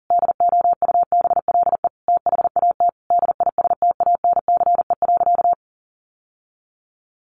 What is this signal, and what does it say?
Morse code, heard as a keyed tone, about 40 wpm, 700 Hz. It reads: DOUBLE THAT DISTANCE.
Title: Sentences from Top 1400 Words 40wpm